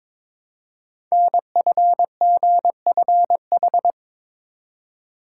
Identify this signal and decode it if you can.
Morse code, heard as a keyed tone, about 22 wpm, 700 Hz. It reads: NFGFH
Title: Code Groups - 5 Letters - 22wpm